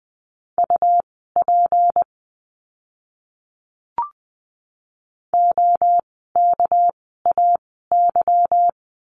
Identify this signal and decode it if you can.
Morse code, heard as a keyed tone, about 20 wpm, spaced as for 14 wpm, 700 Hz. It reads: UP OKAY
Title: 199 Lesson 32f - Letter k - Words 20wpm